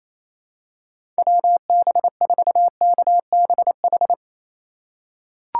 Morse code, heard as a keyed tone, about 28 wpm, 700 Hz. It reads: WB4KBH
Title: USA Call Signs - 6 Characters - 3x Character Spacing - 28wpm